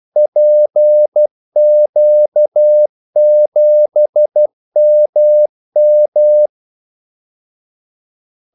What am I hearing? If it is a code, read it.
Morse code, heard as a keyed tone, about 12 wpm, 600 Hz. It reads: PQ7MM
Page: 2 ekstra ordmellemrum mellem kaldesignalerne